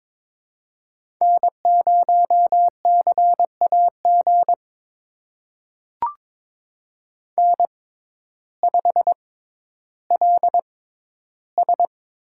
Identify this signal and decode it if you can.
Morse code, heard as a keyed tone, about 22 wpm, spaced as for 12 wpm, 700 Hz. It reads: N0CAG N5LS
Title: USA Call Signs - 5 Characters - 6x Character Spacing - 22wpm